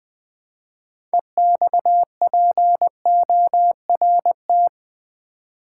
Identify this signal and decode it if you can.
Morse code, heard as a keyed tone, about 20 wpm, 700 Hz. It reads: EXPORT
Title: 6 Letter Words - First 3 Letters Given - 20wpm